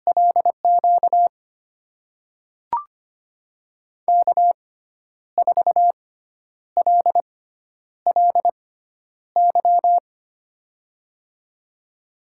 Morse code, heard as a keyed tone, about 25 wpm, spaced as for 10 wpm, 700 Hz. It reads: LQ K4LLY
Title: USA Call Signs - 5 Characters - 6x Character Spacing - 25wpm